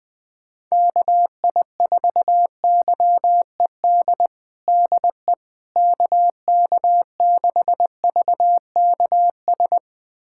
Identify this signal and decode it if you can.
Morse code, heard as a keyed tone, about 20 wpm, 700 Hz. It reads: KI4YED DE KK6VKS
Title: USA Callsign DE Callsign - 6 Characters - 20wpm